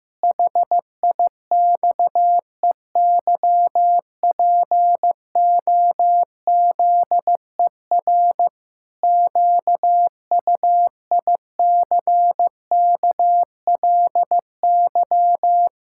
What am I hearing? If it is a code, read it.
Morse code, heard as a keyed tone, about 16 wpm, 700 Hz. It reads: HIXEYPOZER QUICKLY
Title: Pangram Challenge v2 - 15wpm